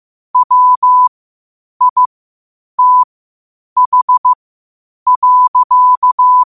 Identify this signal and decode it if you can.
Morse code, heard as a keyed tone, about 15 wpm, spaced as for 8 wpm, 1000 Hz. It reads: WITH.
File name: qotdfiles / 8wpm